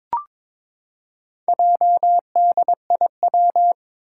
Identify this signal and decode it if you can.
Morse code, heard as a keyed tone, about 23 wpm, 700 Hz. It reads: JDIW
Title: Four Challenging Letter-Number Combinations 22wpm